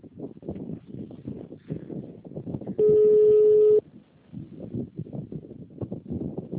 Ringback tone (Europe, 425 Hz).
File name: out- -2007- - - .1058